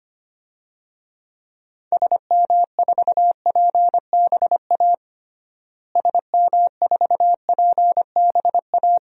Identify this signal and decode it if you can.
Morse code, heard as a keyed tone, about 25 wpm, 700 Hz. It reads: SM4PBA SM4PBA
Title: DXCC-100 Call Signs - 6 Characters - Repeated 3x - 25wpm